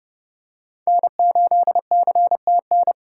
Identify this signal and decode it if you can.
Morse code, heard as a keyed tone, about 30 wpm, 700 Hz. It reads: N8CTN